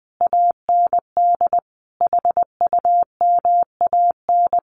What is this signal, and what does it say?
Morse code, heard as a keyed tone, about 20 wpm, 700 Hz. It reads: AND HUMAN
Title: Sets of 5 Words 20wpm